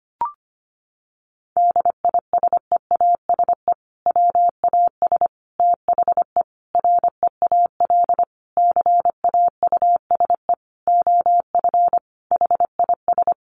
Morse code, heard as a keyed tone, about 25 wpm, 700 Hz. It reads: DISEASE WAS THE REAL CAUSE OF HIS
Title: Sentences from Top 1100 Words 25wpm